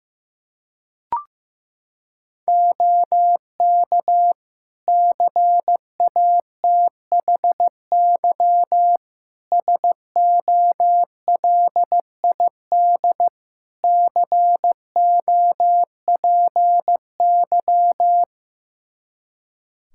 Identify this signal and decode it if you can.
Morse code, heard as a keyed tone, about 16 wpm, 700 Hz. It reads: OK CATHY SOLID COPY
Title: K1USN SST QSO Practice 15wpm